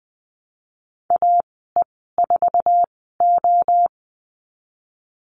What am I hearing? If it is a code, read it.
Morse code, heard as a keyed tone, about 20 wpm, spaced as for 15 wpm, 700 Hz. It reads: AE4O
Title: 055 Lesson 10g - Letter r - Call Signs 20wpm